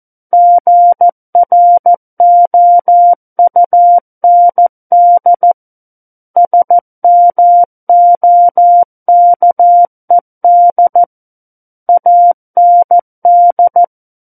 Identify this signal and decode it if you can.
Morse code, heard as a keyed tone, about 14 wpm, 700 Hz. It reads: GROUND SMOKED AND
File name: Book One / 05-Chapter 5 - 14 WPM